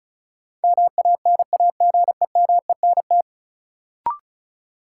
Morse code, heard as a keyed tone, about 34 wpm, 700 Hz. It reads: MANAGEMENT
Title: Sentences from Top 1000 Words - Review Part 4 - New Word Drill - 35wpm